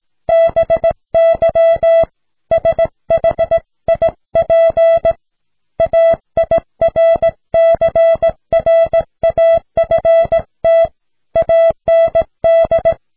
Morse code, heard as a keyed tone, about 17 wpm, 660 Hz. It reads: BY SHIP AIRCRAFT AND